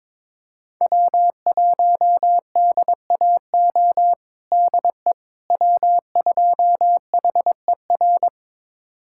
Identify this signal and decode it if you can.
Morse code, heard as a keyed tone, about 21 wpm, 700 Hz. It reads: W1DAO DE W2HER
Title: USA Callsign DE Callsign - 5 Characters - 22wpm